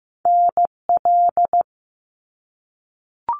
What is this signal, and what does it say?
Morse code, heard as a keyed tone, about 15 wpm, 700 Hz. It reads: NL
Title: Canadian Provinces and Territories 15wpm